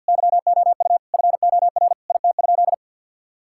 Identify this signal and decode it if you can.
Morse code, heard as a keyed tone, about 50 wpm, 700 Hz. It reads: YOU FOR IT?
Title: Sentences from Top 1000 Words - Review Part 6 - 50wpm